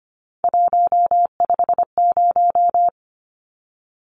Morse code, heard as a keyed tone, about 25 wpm, 700 Hz. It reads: 150
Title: Numbers - 3-Digit 25wpm